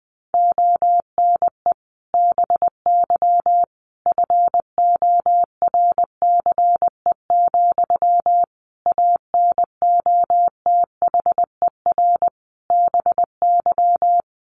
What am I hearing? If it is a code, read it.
Morse code, heard as a keyed tone, about 20 wpm, 700 Hz. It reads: ONE BY FORCE, ANOTHER BY
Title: Sentences from Top 1200 Words 20wpm